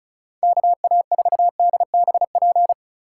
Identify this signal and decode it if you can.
Morse code, heard as a keyed tone, about 36 wpm, 700 Hz. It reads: KA4DBP
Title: USA Call Signs - 6 Characters - 4x Character Spacing - 35wpm